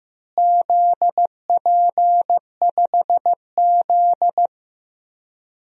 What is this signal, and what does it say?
Morse code, heard as a keyed tone, about 15 wpm, 700 Hz. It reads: ZP5Z
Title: DXCC-100 Call Signs - 4 Characters - Repeated 2x - 15wpm